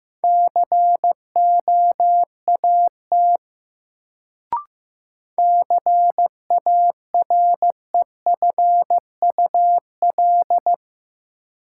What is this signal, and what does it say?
Morse code, heard as a keyed tone, about 15 wpm, 700 Hz. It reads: COAT CAREFUL
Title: Sentences from Top 1600 Words - New Word Drill - 15wpm